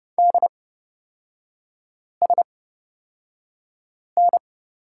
Morse code, heard as a keyed tone, about 30 wpm, spaced as for 5 wpm, 700 Hz. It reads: DSN
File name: Lesson 37e - Letter x - Three Characters 30wpm